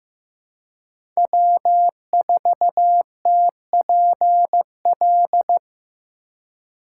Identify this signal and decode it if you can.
Morse code, heard as a keyed tone, about 15 wpm, 700 Hz. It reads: W4TPL
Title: USA Call Signs - 5 Characters - 4x Character Spacing - 15wpm